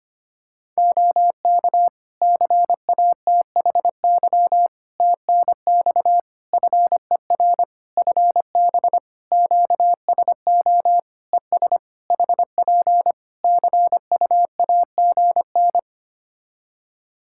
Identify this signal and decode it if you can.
Morse code, heard as a keyed tone, about 25 wpm, 700 Hz. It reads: OK CATHY TNX FER FB QSO ES HP CUAGN
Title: K1USN SST QSO Practice 25wpm